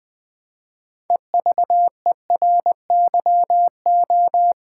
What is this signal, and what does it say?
Morse code, heard as a keyed tone, about 20 wpm, 700 Hz. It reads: EVERYO
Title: Build A Word - 2 Letters At A Time - 8-Letter Words - 20wpm